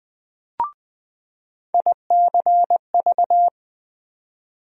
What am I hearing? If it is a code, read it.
Morse code, heard as a keyed tone, about 20 wpm, 700 Hz. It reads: ICV
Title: USA Call Sign Postfixes - 3 Letters 20wpm